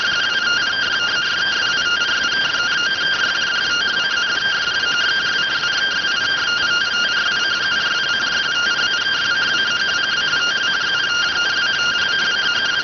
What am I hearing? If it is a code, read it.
Two-tone teleprinter signal, about 170 Hz shift.